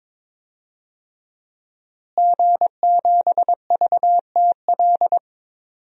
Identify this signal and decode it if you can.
Morse code, heard as a keyed tone, about 22 wpm, 700 Hz. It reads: G7VTL
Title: DXCC-100 Call Signs - 5 Characters - Repeated 3x - 22wpm